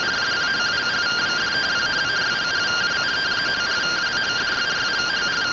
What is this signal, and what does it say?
Two-tone teleprinter signal, about 170 Hz shift.